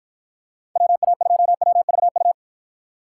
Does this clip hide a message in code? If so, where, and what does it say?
Yes: Morse code from 0.8 to 2.3 s, heard as a keyed tone, about 51 wpm, 700 Hz. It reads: WA2WVU